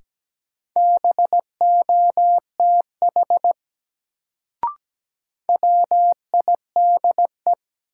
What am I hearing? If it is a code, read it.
Morse code, heard as a keyed tone, about 17 wpm, 700 Hz. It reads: BOTH WIDE
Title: Sets of 4 Words - New Word Drill - 17wpm